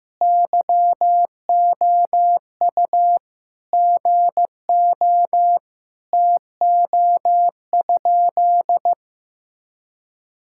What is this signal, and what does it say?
Morse code, heard as a keyed tone, about 15 wpm, 700 Hz. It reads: YOU GO TO?
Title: Sentences from Top 1000 Words - Review Part 5 - 15wpm